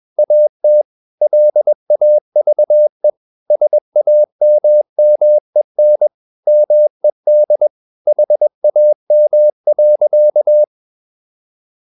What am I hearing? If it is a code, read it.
Morse code, heard as a keyed tone, about 21 wpm, 600 Hz. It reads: AT LAVE SAMMEN MED HAM.
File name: Ferie QRS 21wpm